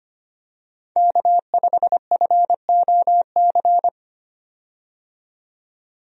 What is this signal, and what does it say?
Morse code, heard as a keyed tone, about 25 wpm, 700 Hz. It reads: K5FOC
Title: USA Call Signs - 5 Characters - 25wpm